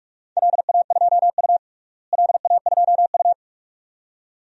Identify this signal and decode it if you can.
Morse code, heard as a keyed tone, about 46 wpm, 700 Hz. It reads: LA2U LA2U
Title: DXCC-100 Call Signs - 4 Characters - Repeated 2x - 45wpm